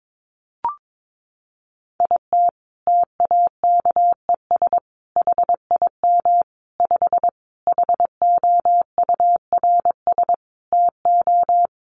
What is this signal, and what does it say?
Morse code, heard as a keyed tone, about 22 wpm, 700 Hz. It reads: IT TAKES HIM 5 HOURS TO
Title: Additional Easy Phrases 22wpm